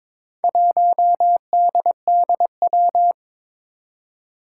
Morse code, heard as a keyed tone, about 22 wpm, 700 Hz. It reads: 1DDW
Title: Four Challenging Letter-Number Combinations 22wpm